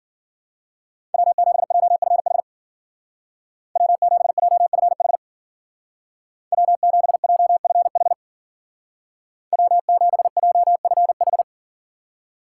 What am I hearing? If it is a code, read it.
Morse code, heard as a keyed tone, about 48 wpm, 700 Hz. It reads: W7JFH W7JFH W7JFH W7JFH
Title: Speed Racing - USA 1x3 Call Signs 40wpm